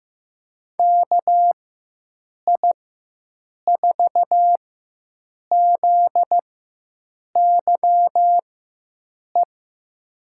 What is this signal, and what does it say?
Morse code, heard as a keyed tone, about 15 wpm, spaced as for 7 wpm, 700 Hz. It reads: KI4ZYE
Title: USA Call Signs - 6 Characters - 4x Character Spacing - 15wpm